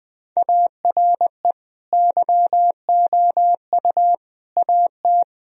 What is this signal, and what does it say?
Morse code, heard as a keyed tone, about 20 wpm, 700 Hz. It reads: ARE YOU AT
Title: Sentences from Top 1000 Words - Review Part 4 - 20wpm